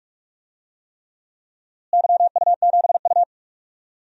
Morse code, heard as a keyed tone, about 46 wpm, 700 Hz. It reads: YU7U
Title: DXCC-100 Call Signs - 4 Characters - Repeated 3x - 45wpm